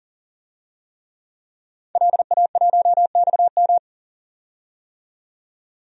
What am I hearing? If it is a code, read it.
Morse code, heard as a keyed tone, about 40 wpm, 700 Hz. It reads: LA1XM
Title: DXCC-100 Call Signs - 5 Characters - 40wpm